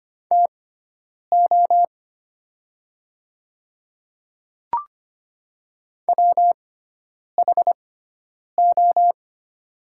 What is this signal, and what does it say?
Morse code, heard as a keyed tone, about 25 wpm, spaced as for 9 wpm, 700 Hz. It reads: TO WHO
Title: 6x Character Spacing - Top 300 Words - 25wpm